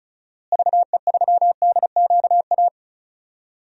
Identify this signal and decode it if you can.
Morse code, heard as a keyed tone, about 36 wpm, 700 Hz. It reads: VE3DQA
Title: DXCC-100 Call Signs - 6 Characters - Repeated 3x - 35wpm